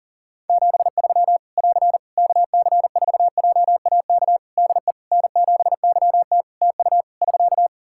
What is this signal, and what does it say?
Morse code, heard as a keyed tone, about 40 wpm, 700 Hz. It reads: 73 <AR> KC4JAK DE N7YT TU <SK>
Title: Practice QSO - Exchange 4 - Ending - 40wpm